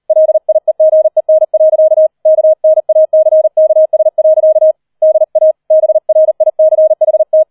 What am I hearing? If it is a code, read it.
Morse code, heard as a keyed tone, about 38 wpm, 630 Hz. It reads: LIEGEN. KNACKS. DA BRICHT